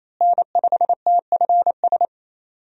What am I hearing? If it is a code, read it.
Morse code, heard as a keyed tone, about 28 wpm, 700 Hz. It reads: N5TFS
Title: USA Call Signs - 5 Characters - Repeated 2x - 28wpm